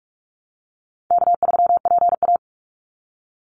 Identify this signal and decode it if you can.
Morse code, heard as a keyed tone, about 44 wpm, 700 Hz. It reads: K3PA
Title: Easy Call Signs - 45wpm